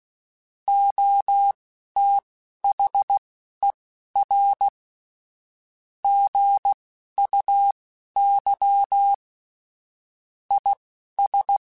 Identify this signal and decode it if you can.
Morse code, heard as a keyed tone, about 16 wpm, spaced as for 11 wpm, 800 Hz. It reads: OTHER GUY IS
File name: Chuckles / Chuckles-mp3s / HumanErr @10